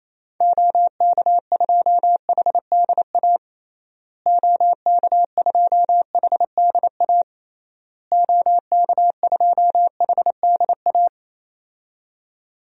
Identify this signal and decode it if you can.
Morse code, heard as a keyed tone, about 28 wpm, 700 Hz. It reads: OK2HDA OK2HDA OK2HDA
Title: DXCC-100 Call Signs - 6 Characters - Repeated 3x - 28wpm